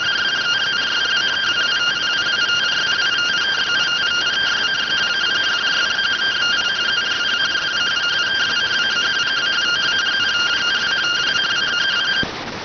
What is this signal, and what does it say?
Two-tone teleprinter signal, about 170 Hz shift.